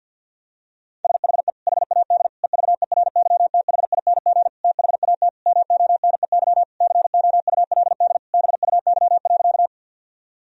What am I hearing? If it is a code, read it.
Morse code, heard as a keyed tone, about 51 wpm, 700 Hz. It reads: SHE HAD EVERYTHING THAT MONEY COULD BUY.
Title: Sentences from Top 1000 Words - Review Part 4 - 50wpm